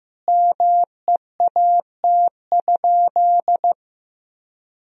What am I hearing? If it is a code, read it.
Morse code, heard as a keyed tone, about 15 wpm, 700 Hz. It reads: MEAT?
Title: Sentences from Top 1000 Words - Review Part 3 - 15wpm